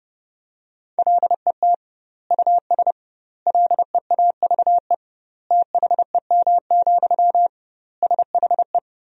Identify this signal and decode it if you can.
Morse code, heard as a keyed tone, about 30 wpm, 700 Hz. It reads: LET US LEAVE THEM, SHE
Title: 2x Word-Spacing - Sentences from Top 400 Words - 30wpm